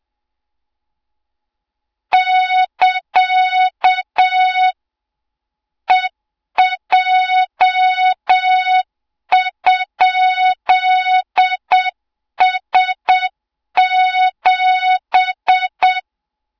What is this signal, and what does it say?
Morse code, heard as a keyed tone, about 7 wpm, 3700 Hz, with noise. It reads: <KA> EJ?S7
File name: Morze-HA2MN / hang / 20 GYAKORLAT